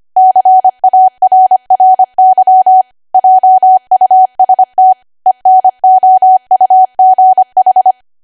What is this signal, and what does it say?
Morse code, heard as a keyed tone, about 25 wpm, 750 Hz. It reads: CARRY JUST ENOUGH
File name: Archive / 25 WPM / 25WPM